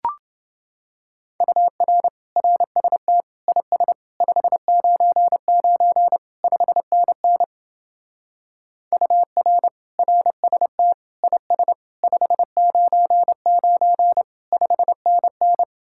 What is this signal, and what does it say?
Morse code, heard as a keyed tone, about 29 wpm, 700 Hz. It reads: UR RST IS 599 5NN UR RST IS 599 5NN